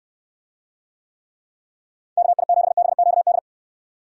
Morse code, heard as a keyed tone, about 69 wpm, 700 Hz. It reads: KI7BQD